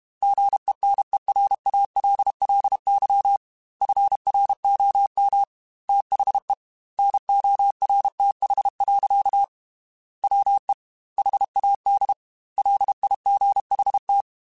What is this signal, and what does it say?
Morse code, heard as a keyed tone, about 32 wpm, 800 Hz. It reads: GENERALLY FROM THE NORTH. WE HAD LIGHT
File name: Weather Sets / Weather Sets-mp3s / Weather 4 24 @30